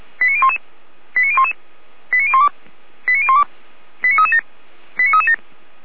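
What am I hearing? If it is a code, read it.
Five-tone selective call (ZVEI), heard as quick runs of five tones, tones of about 70 ms. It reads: 89011 89011 89012 89012 89337 89337
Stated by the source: ZVEI five-tone selective call received on an SDR.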